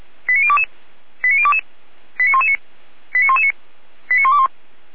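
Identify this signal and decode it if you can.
Five-tone selective call (ZVEI), heard as quick runs of five tones, tones of about 70 ms. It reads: 89022 89022 89119 89119 89121
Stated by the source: ZVEI five-tone selective call 89017 received on SDR.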